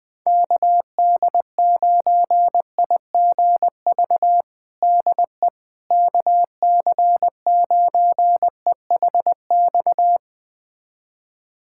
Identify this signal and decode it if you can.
Morse code, heard as a keyed tone, about 20 wpm, 700 Hz. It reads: KD9IGV DE KC9EHX
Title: USA Callsign DE Callsign - 6 Characters - 20wpm